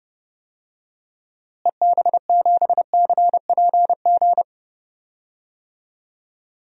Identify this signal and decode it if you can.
Morse code, heard as a keyed tone, about 30 wpm, 700 Hz. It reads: EB7CPG
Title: DXCC-100 Call Signs - 6 Characters - Repeated 2x - 30wpm